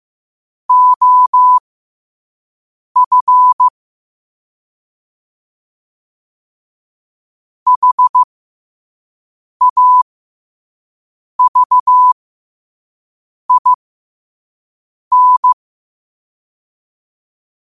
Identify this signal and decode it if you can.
Morse code, heard as a keyed tone, about 15 wpm, spaced as for 5 wpm, 1000 Hz. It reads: OF HAVIN
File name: qotdfiles / 5wpm